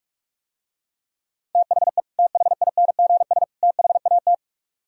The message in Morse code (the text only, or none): THE THINGS THAT